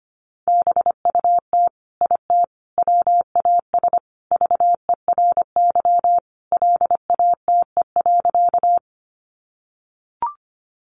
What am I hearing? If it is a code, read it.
Morse code, heard as a keyed tone, about 25 wpm, 700 Hz. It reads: BUT IT WAS VERY LATE.
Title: Sentences from Top 300 Words 25wpm